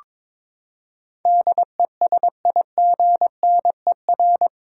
Morse code, heard as a keyed tone, about 22 wpm, 700 Hz. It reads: DESIGNER
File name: practice / 22wpm / Eight Letter Words - 22wpm